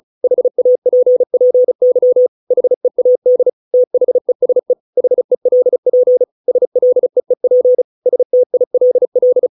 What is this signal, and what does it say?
Morse code, heard as a keyed tone, about 34 wpm, 500 Hz. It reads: HAPPY HEAD THESE HELP SLEEP STILL